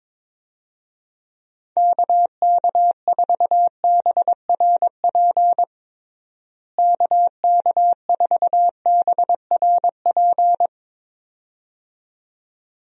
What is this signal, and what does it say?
Morse code, heard as a keyed tone, about 22 wpm, 700 Hz. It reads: KK4BRP KK4BRP
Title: USA Call Signs - 6 Characters - Repeated 2x - 22wpm